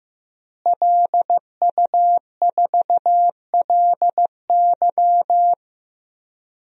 Morse code, heard as a keyed tone, about 16 wpm, 700 Hz. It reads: LU4LY
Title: DXCC-100 Call Signs - 5 Characters - Repeated 3x - 15wpm